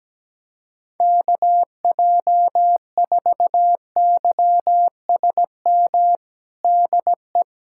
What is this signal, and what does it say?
Morse code, heard as a keyed tone, about 17 wpm, 700 Hz. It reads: KJ4YSM DE
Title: USA Callsign DE Callsign - 6 Characters - 17wpm